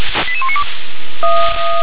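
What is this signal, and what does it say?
Five-tone selective call (ZVEI), heard as quick runs of five tones, tones of about 70 ms. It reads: 88192
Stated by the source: ZVEI five-tone selective call 88154 received by SDR.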